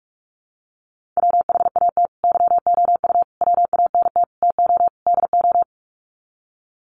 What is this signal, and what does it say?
Morse code, heard as a keyed tone, about 44 wpm, 700 Hz. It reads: WHAT YOU WANT TO DO
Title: Sets of 5 Words Encore 45wpm